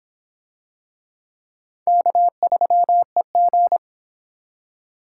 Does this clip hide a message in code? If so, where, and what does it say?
Yes: Morse code from 1.9 to 3.8 s, heard as a keyed tone, about 25 wpm, 700 Hz. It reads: K3EG